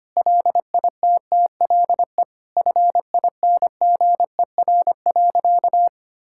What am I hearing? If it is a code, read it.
Morse code, heard as a keyed tone, about 25 wpm, 700 Hz. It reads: LITTLE FINGER.
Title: Sentences from Top 1600 Words 25wpm